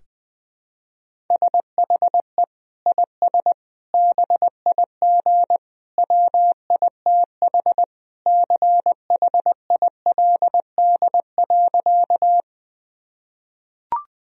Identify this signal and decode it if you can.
Morse code, heard as a keyed tone, about 20 wpm, 700 Hz. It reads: SHE IS BIG WITH CHILD.
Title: Sentences from Top 1000 Words - Review Part 1 - 20wpm